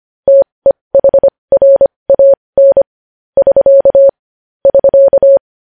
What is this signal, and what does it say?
Morse code, heard as a keyed tone, about 25 wpm, spaced as for 20 wpm, 550 Hz. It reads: TEHRAN <SK> <SK>
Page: news headlines in morse code at 20 wpm